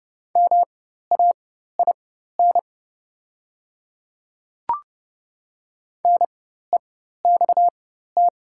Morse code, heard as a keyed tone, about 31 wpm, spaced as for 14 wpm, 690 Hz. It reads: MAIN NEXT
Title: 4x Character Spacing - Top 300 Words - 30wpm